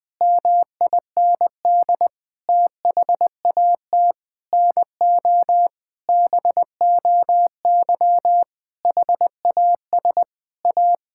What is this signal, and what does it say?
Morse code, heard as a keyed tone, about 20 wpm, 700 Hz. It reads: MIND THAT NO BOY HAS A
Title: Sentences from Top 1000 Words - Review Part 6 - 20wpm